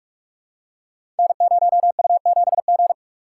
Morse code, heard as a keyed tone, about 44 wpm, 700 Hz. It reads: N0U7G
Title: Code Groups - 5 Letter:Numbers - 45wpm